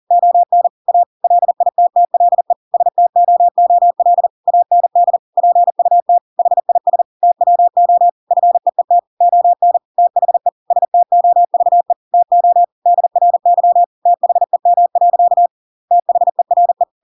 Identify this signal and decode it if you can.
Morse code, heard as a keyed tone, about 40 wpm, 700 Hz. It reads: ON A LITTLE STOOL AND PUT HIS TWO FEET ON THE STOVE TO DRY THEM. THERE
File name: Pinocchio / Pinocchio ch6 40 40 ik1hge